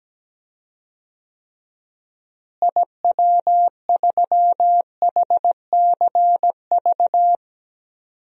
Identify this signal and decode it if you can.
Morse code, heard as a keyed tone, about 17 wpm, 700 Hz. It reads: IW3HCV
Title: DXCC-100 Call Signs - 6 Characters - Repeated 2x - 17wpm